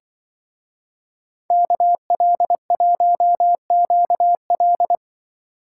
Morse code, heard as a keyed tone, about 27 wpm, 700 Hz. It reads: KL1QL